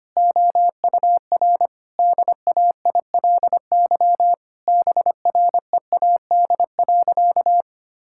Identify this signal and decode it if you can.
Morse code, heard as a keyed tone, about 25 wpm, 700 Hz. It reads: OUR DAILY BREAD.
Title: Sentences from Top 1600 Words 25wpm